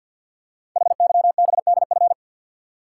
Morse code, heard as a keyed tone, about 53 wpm, 700 Hz. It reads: HYBDF
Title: Code Groups - 5 Letter:Numbers - 50wpm